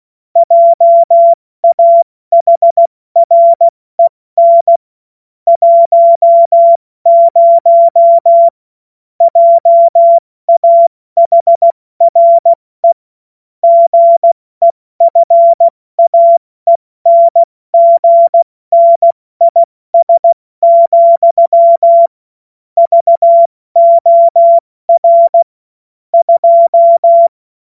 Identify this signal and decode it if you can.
Morse code, heard as a keyed tone, about 16 wpm, 680 Hz. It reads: JAHREN 10 JAHRE GEFAENGNIS, VOR 2
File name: Morsekurs / Witze / Witz 19. April 2022 70 F80